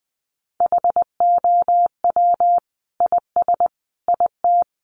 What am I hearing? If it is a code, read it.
Morse code, heard as a keyed tone, about 20 wpm, 700 Hz. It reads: HOW IS IT
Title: Sentences from Top 900 Words 20wpm